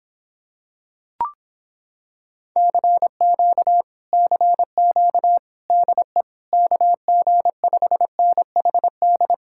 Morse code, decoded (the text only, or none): CQ CQ DE KG5NHD